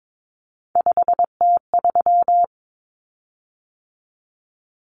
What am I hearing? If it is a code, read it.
Morse code, heard as a keyed tone, about 22 wpm, 700 Hz. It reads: 5T3
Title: Common Cut Numbers - 3-Digit 22wpm